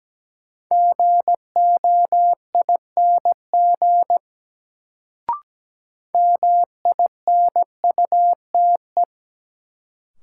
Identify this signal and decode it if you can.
Morse code, heard as a keyed tone, about 17 wpm, 700 Hz. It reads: GOING MINUTE
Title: Sentences from Top 500 Words - New Word Drill - 17wpm